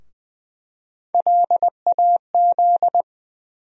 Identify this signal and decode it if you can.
Morse code, heard as a keyed tone, about 20 wpm, 700 Hz. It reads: LAZ
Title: USA Call Sign Postfixes - 3 Letters 20wpm